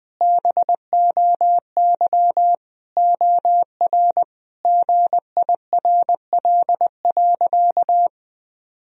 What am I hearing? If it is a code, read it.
Morse code, heard as a keyed tone, about 20 wpm, 700 Hz. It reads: BOY OR GIRL.
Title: Sentences from Top 1000 Words - Review Part 3 - 20wpm